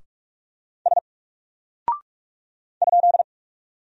Morse code, heard as a keyed tone, about 44 wpm, 700 Hz. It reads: S ?
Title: Single Character (Letter:Number:Punctuation:Prosign) Alt 45wpm